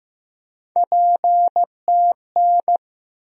Morse code, heard as a keyed tone, about 15 wpm, 700 Hz. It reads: PTN
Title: USA Call Sign Postfixes - 3 Letters 15wpm